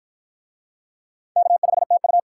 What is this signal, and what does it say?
Morse code, heard as a keyed tone, about 53 wpm, 700 Hz. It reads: K5TU